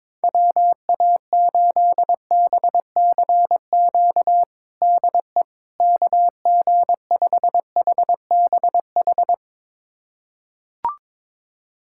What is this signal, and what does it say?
Morse code, heard as a keyed tone, about 21 wpm, 700 Hz. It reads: WA8BCQ DE KG5HBH
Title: USA Callsign DE Callsign - 6 Characters - 22wpm